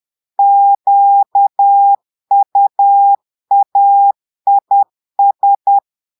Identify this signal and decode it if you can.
Morse code, heard as a keyed tone, about 10 wpm, 800 Hz. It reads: QUAIS
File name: portugais texte en clair 2 vitesse 10 mots minutes